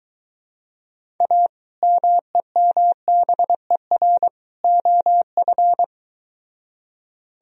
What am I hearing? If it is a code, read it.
Morse code, heard as a keyed tone, about 23 wpm, 700 Hz. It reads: A MEMBER OF